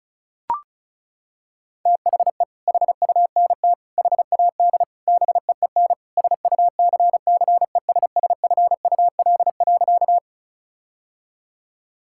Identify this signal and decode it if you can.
Morse code, heard as a keyed tone, about 34 wpm, 700 Hz. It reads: THE HUNT HAD BEEN SUCCESSFUL.
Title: Sentences from Top 1600 Words 35wpm